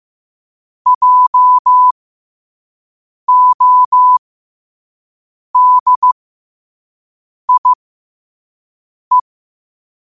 Morse code, heard as a keyed tone, about 15 wpm, spaced as for 5 wpm, 1000 Hz. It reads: JODIE
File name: qotdfiles / 5wpm